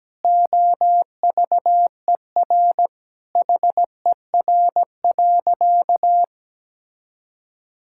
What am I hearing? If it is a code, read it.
Morse code, heard as a keyed tone, about 17 wpm, 700 Hz. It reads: OVER HER.
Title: Sentences from Top 1000 Words - Review Part 5 - 17wpm